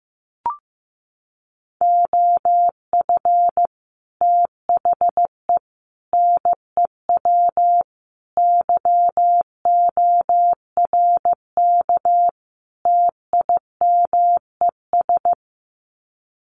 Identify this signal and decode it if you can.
Morse code, heard as a keyed tone, about 15 wpm, 700 Hz. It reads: OF THE NEW YORK TIMES
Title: Sets of 5 Words 15wpm